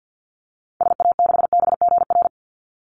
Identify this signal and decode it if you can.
Morse code, heard as a keyed tone, about 50 wpm, 700 Hz. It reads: SA6BGR
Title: Call Signs - 50wpm